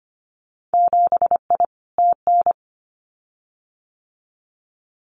Morse code, heard as a keyed tone, about 25 wpm, 700 Hz. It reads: 7I TN
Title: Winter Field Day Exchanges Encore 25wpm